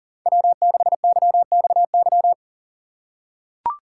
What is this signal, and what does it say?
Morse code, heard as a keyed tone, about 40 wpm, 700 Hz. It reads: W6YXY
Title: USA Call Signs - 5 Characters - 3x Character Spacing - 40wpm